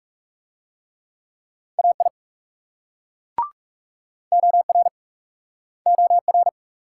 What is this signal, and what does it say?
Morse code, heard as a keyed tone, about 40 wpm, 700 Hz. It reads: AI OR OR